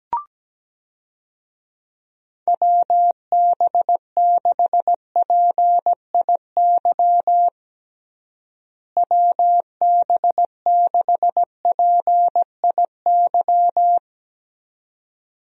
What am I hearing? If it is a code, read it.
Morse code, heard as a keyed tone, about 17 wpm, 700 Hz. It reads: WB6PIY WB6PIY
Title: USA Call Signs - 6 Characters - Repeated 2x - 17wpm